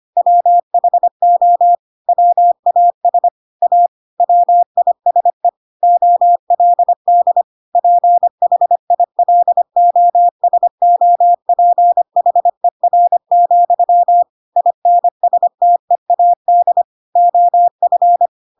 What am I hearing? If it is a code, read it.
Morse code, heard as a keyed tone, about 25 wpm, 700 Hz. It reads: WHO WAS A WISE OLD PHILOSOPHER, INSTEAD OF
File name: Pinocchio / Pinocchio ch4 25 25 ik1hge